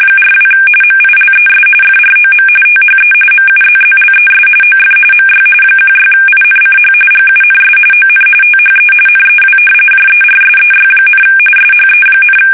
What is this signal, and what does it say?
RTTY, about 45.45 baud. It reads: 42 MILES FROM THE MUZZLES OF MY 16INCH GUNS ON THE NEW JERSEY, SHE,
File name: fsk850 / battleshipnj